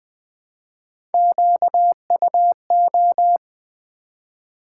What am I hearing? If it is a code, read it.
Morse code, heard as a keyed tone, about 20 wpm, 700 Hz. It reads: QUO
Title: Build A Word - 1 Letter At A Time - 5-Letter Words - 20wpm